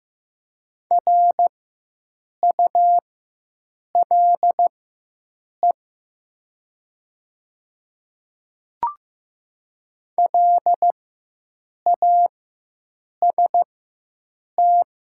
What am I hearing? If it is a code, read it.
Morse code, heard as a keyed tone, about 15 wpm, spaced as for 7 wpm, 700 Hz. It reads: RULE LAST
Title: 4x Character Spacing - Top 500 Words - 15wpm